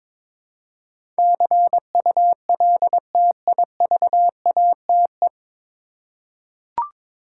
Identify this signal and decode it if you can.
Morse code, heard as a keyed tone, about 22 wpm, 700 Hz. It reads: CULTIVATE
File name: 9 Letter Words - First 3 Letters Given - 22wpm